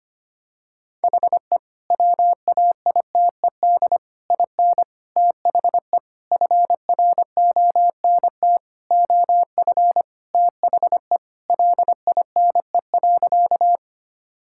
Morse code, heard as a keyed tone, about 25 wpm, 700 Hz. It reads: HE WAITED IN THE FRONT OF THE LINE.
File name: practice / 25wpm / Sentences with 4-Word Phrases - 25wpm